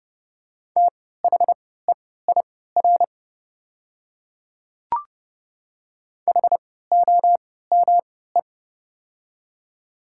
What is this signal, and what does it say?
Morse code, heard as a keyed tone, about 30 wpm, spaced as for 17 wpm, 700 Hz. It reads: THEIR HOME
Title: 3x Character Spacing - Top 300 Words - 30wpm